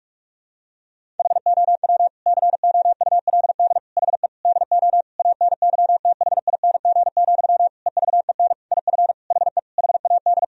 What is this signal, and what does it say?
Morse code, heard as a keyed tone, about 41 wpm, 700 Hz. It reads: HOW COULD HE DO ANYTHING, EVEN IF HE HAD
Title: Sentences from Top 1000 Words - Review Part 1 - 45wpm